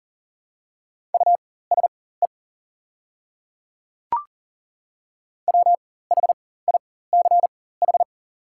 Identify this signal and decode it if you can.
Morse code, heard as a keyed tone, about 39 wpm, spaced as for 19 wpm, 700 Hz. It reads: USE WHICH
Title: 4x Character Spacing - Top 100 Words - 40wpm